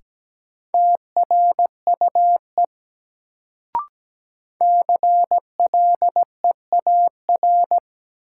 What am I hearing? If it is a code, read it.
Morse code, heard as a keyed tone, about 17 wpm, 700 Hz. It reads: TRUE CLEAR
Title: Sets of 5 Words Encore - New Word Drill - 17wpm